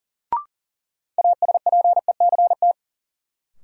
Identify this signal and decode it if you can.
Morse code, heard as a keyed tone, about 40 wpm, 700 Hz. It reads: ASPECT
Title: Sentences from Top 1300 Words - New Word Drill Alt - 40wpm